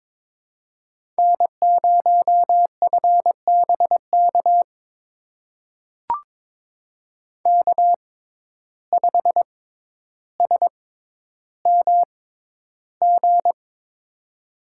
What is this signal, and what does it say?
Morse code, heard as a keyed tone, about 22 wpm, spaced as for 11 wpm, 700 Hz. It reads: N0FBK K5SMG
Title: USA Call Signs - 5 Characters - 6x Character Spacing - 22wpm